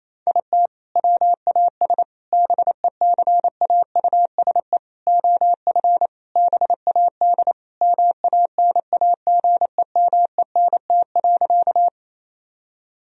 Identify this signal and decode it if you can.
Morse code, heard as a keyed tone, about 27 wpm, 700 Hz. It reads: IT WAS BECAUSE OF BAD MANAGEMENT.
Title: Sentences from Top 1000 Words - Review Part 4 - 28wpm